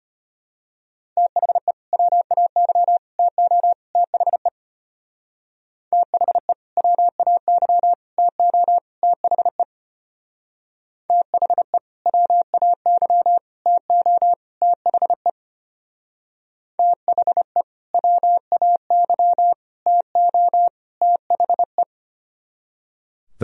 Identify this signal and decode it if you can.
Morse code, heard as a keyed tone, about 30 wpm, 700 Hz. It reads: THE WAY TO THE THE WAY TO THE THE WAY TO THE THE WAY TO THE
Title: Speed-Racing - Sets of 4 Words 25wpm